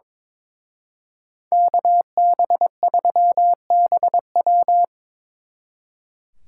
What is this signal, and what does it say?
Morse code, heard as a keyed tone, about 26 wpm, 700 Hz. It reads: KB3BW
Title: Speed Racing - USA Call Signs 22wpm